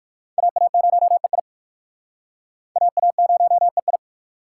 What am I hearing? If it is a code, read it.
Morse code, heard as a keyed tone, about 51 wpm, 700 Hz. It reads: AA0EI AA0EI
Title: Speed Racing - USA 2x2 Call Signs 45wpm